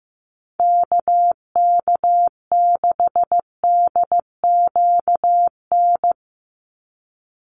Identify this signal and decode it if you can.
Morse code, heard as a keyed tone, about 15 wpm, 700 Hz. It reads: KK6DQN
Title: USA Call Signs 15wpm